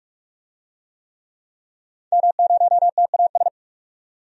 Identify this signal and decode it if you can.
Morse code, heard as a keyed tone, about 41 wpm, 700 Hz. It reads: M0TAS
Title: DXCC-100 Call Signs - 5 Characters - 45wpm